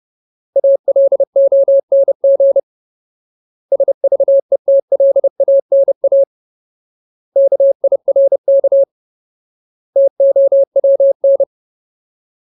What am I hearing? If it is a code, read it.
Morse code, heard as a keyed tone, about 30 wpm, 550 Hz. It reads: ALONG SVETLANA KIRK TOWN
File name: Ultimate Word List / 30WPM Ultimate